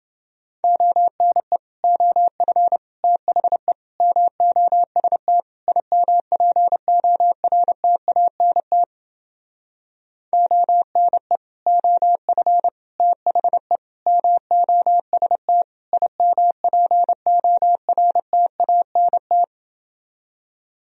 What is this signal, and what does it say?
Morse code, heard as a keyed tone, about 30 wpm, 700 Hz. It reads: ONE OF THE MOST IMPORTANT ONE OF THE MOST IMPORTANT